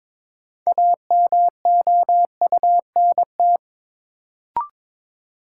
Morse code, heard as a keyed tone, about 22 wpm, 700 Hz. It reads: AMOUNT
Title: Sentences from Top 700 Words - New Word Drill - 22wpm